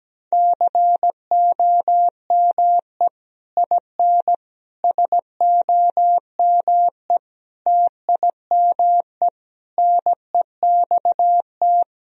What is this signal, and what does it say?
Morse code, heard as a keyed tone, about 17 wpm, 700 Hz. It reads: COME IN SOME TIME NEXT
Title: Sentences from Top 1000 Words - Review Part 2 - 17wpm